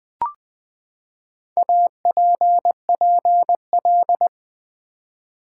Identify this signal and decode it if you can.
Morse code, heard as a keyed tone, about 20 wpm, 700 Hz. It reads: APPL
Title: Build A Word - 4 Letters At A Time - 10-Letter Words - 20wpm